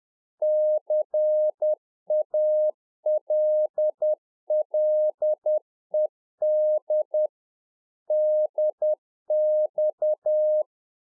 Morse code, heard as a keyed tone, about 10 wpm, 610 Hz. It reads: CALLED DX